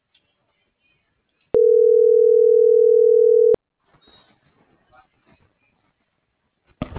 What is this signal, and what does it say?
Ringback tone (North America), 440 + 480 Hz.